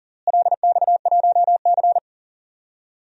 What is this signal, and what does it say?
Morse code, heard as a keyed tone, about 40 wpm, 700 Hz. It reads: LX1C
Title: DXCC-100 Call Signs - 4 Characters - Repeated 2x - 40wpm